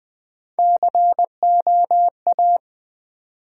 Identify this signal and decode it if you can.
Morse code, heard as a keyed tone, about 20 wpm, 700 Hz. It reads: COA
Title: Build A Word - 1 Letter At A Time - 3 to 5-Letter Words - 20wpm